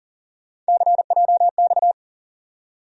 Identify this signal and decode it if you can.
Morse code, heard as a keyed tone, about 37 wpm, 700 Hz. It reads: CJX
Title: USA Call Sign Postfixes - 3 Letters 40wpm